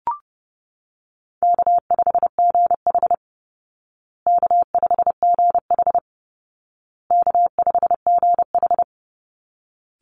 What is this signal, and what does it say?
Morse code, heard as a keyed tone, about 30 wpm, 700 Hz. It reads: K5GH K5GH K5GH
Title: Call Signs Repeated 2x - 30wpm